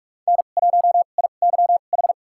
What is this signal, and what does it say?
Morse code, heard as a keyed tone, about 45 wpm, spaced as for 32 wpm, 700 Hz. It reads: N1IYH
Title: USA Call Signs - 5 Characters - 2x Character Spacing - 45wpm